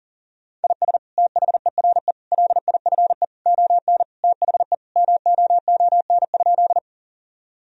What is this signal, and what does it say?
Morse code, heard as a keyed tone, about 40 wpm, 700 Hz. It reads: IS THERE LIFE ON THE MOON?
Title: Sentences from Top 1000 Words - Review Part 3 - 40wpm